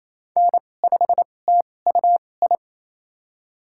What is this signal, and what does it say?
Morse code, heard as a keyed tone, about 28 wpm, spaced as for 20 wpm, 700 Hz. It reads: N5TUI
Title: USA Call Signs - 2x Character Spacing - 28wpm